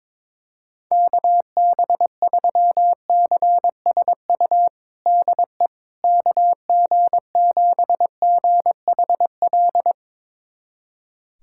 Morse code, heard as a keyed tone, about 22 wpm, 700 Hz. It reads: KB3CSU DE KG7GHL
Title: USA Callsign DE Callsign - 6 Characters - 22wpm